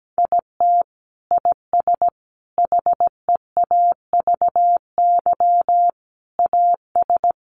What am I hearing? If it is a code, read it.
Morse code, heard as a keyed tone, about 17 wpm, 700 Hz. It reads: IT IS HEAVY AS
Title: Sentences from Top 1100 Words 17wpm